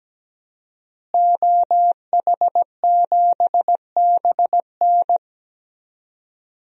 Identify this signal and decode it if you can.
Morse code, heard as a keyed tone, about 17 wpm, 700 Hz. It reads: OH7BN
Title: DXCC-100 Call Signs - 5 Characters - Repeated 2x - 17wpm